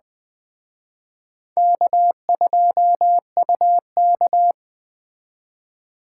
Morse code, heard as a keyed tone, about 20 wpm, 700 Hz. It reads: K2UK